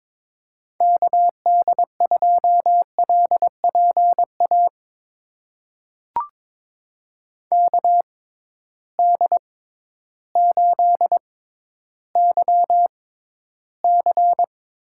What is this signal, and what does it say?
Morse code, heard as a keyed tone, about 22 wpm, spaced as for 12 wpm, 700 Hz. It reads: KD2LPA KD8YC
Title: USA Call Signs - 6 Characters - 6x Character Spacing - 22wpm